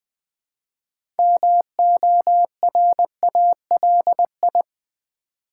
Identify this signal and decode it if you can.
Morse code, heard as a keyed tone, about 20 wpm, 700 Hz. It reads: MORALI
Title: Build A Word - 2 Letters At A Time - 8-Letter Words - 20wpm